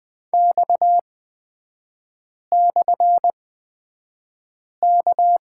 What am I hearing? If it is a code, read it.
Morse code, heard as a keyed tone, about 20 wpm, spaced as for 5 wpm, 690 Hz. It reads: X/K
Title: 229 Lesson 37e - Letter x - Three Characters 20wpm